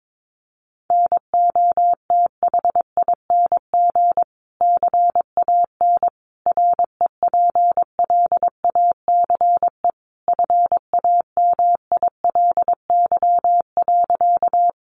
Morse code, heard as a keyed tone, about 22 wpm, 700 Hz. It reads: NOTHING CAN REPLACE FAMILY.
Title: Sentences from Top 1400 Words 22wpm